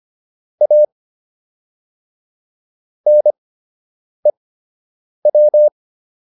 Morse code, heard as a keyed tone, about 25 wpm, spaced as for 8 wpm, 600 Hz. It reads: A NEW